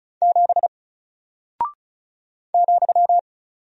Morse code, heard as a keyed tone, about 36 wpm, 700 Hz. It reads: 7 ,
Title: Single Character (Letter:Number:Punctuation:Prosign) Alt 35wpm